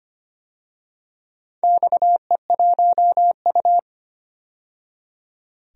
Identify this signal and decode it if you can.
Morse code, heard as a keyed tone, about 25 wpm, 700 Hz. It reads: XE1U
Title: DXCC-100 Call Signs - 4 Characters - 25wpm